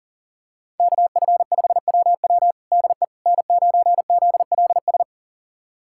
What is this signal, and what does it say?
Morse code, heard as a keyed tone, about 40 wpm, 700 Hz. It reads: KF5WW DE N9ZLS
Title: USA Callsign DE Callsign - 5 Characters - 40wpm